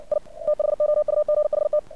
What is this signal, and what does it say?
Morse code, heard as a keyed tone, about 59 wpm, 600 Hz. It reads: I THOUGHT
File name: archived / CRSnet